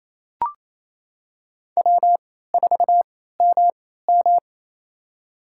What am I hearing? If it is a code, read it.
Morse code, heard as a keyed tone, about 28 wpm, spaced as for 16 wpm, 700 Hz. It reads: W4MM
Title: USA Call Signs - 4 Characters - 3x Character Spacing - 28wpm